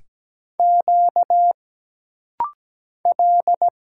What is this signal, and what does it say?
Morse code, heard as a keyed tone, about 17 wpm, 700 Hz. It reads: Q L
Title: Single Letters - Rapid-Fire 17wpm